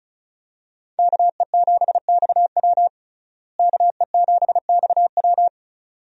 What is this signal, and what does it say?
Morse code, heard as a keyed tone, about 36 wpm, 700 Hz. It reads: KE7XW KE7XW
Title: USA Call Signs - 5 Characters - Repeated 2x - 35wpm